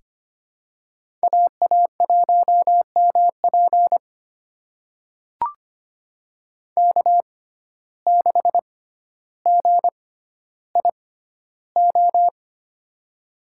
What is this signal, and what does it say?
Morse code, heard as a keyed tone, about 25 wpm, spaced as for 13 wpm, 700 Hz. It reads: AA1MP K6GIO
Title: USA Call Signs - 5 Characters - 6x Character Spacing - 25wpm